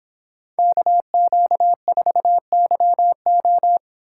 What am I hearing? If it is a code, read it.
Morse code, heard as a keyed tone, about 24 wpm, 700 Hz. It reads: KQ4YO